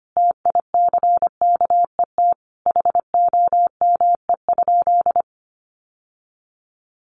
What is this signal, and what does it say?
Morse code, heard as a keyed tone, about 25 wpm, 700 Hz. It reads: TICKET HOME?
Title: Sentences from Top 1400 Words 25wpm